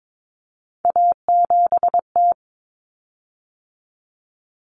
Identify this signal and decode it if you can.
Morse code, heard as a keyed tone, about 22 wpm, 700 Hz. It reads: A7T
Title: Common Cut Numbers - 3-Digit 22wpm